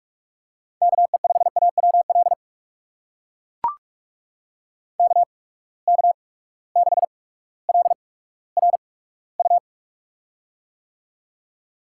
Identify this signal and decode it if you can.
Morse code, heard as a keyed tone, about 42 wpm, spaced as for 19 wpm, 700 Hz. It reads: KE5AWL KK6LRU
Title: USA Call Signs - 8x Character Spacing - 45wpm